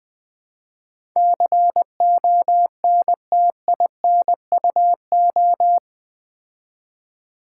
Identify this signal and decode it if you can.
Morse code, heard as a keyed tone, about 20 wpm, 700 Hz. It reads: CONTINUO
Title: Build A Word - 4 Letters At A Time - 10-Letter Words - 20wpm